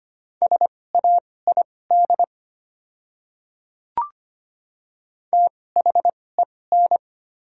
Morse code, heard as a keyed tone, about 25 wpm, spaced as for 18 wpm, 690 Hz. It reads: SAID THEN
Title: 2x Character Spacing - Top 100 Words - 25wpm